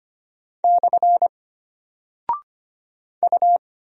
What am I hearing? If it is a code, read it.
Morse code, heard as a keyed tone, about 25 wpm, 700 Hz. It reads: / U
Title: Single Character (Letter:Number:Punctuation:Prosign) Alt 25wpm